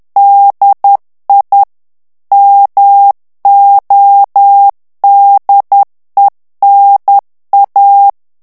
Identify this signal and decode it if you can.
Morse code, heard as a keyed tone, about 10 wpm, 800 Hz. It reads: DI MODENA
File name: morse mp3 / ITALIEN texte clair 1 vitesse 10 mots minute